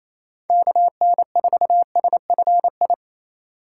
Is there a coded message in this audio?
Yes: Morse code from 0.5 to 2.9 s, heard as a keyed tone, about 37 wpm, 700 Hz. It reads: KN4SFI